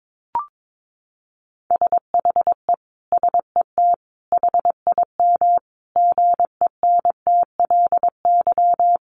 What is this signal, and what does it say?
Morse code, heard as a keyed tone, about 22 wpm, 700 Hz. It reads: SHE SET HIM GENTLY
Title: Sentences from Top 1500 Words 22wpm